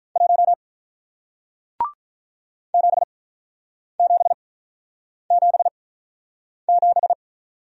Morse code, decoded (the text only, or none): 1 7777